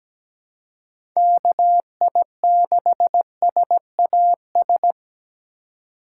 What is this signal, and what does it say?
Morse code, heard as a keyed tone, about 17 wpm, 700 Hz. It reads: KI6SAS
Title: USA Call Signs - 6 Characters - 2x Character Spacing - 17wpm